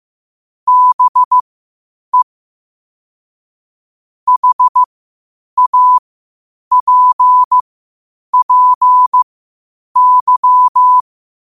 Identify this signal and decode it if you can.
Morse code, heard as a keyed tone, about 15 wpm, spaced as for 8 wpm, 1000 Hz. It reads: BE HAPPY
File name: qotdfiles / 8wpm